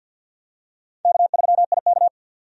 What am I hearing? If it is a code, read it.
Morse code, heard as a keyed tone, about 51 wpm, 700 Hz. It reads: K3IK
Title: USA Call Signs - 4 Characters - 50wpm